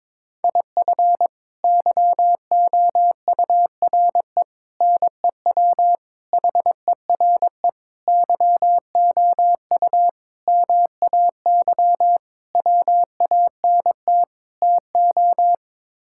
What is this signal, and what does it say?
Morse code, heard as a keyed tone, about 22 wpm, 700 Hz. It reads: IF YOURE NEW HERE YOU MAY WANT TO
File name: practice / 22wpm / Sets of 8 Words - 22wpm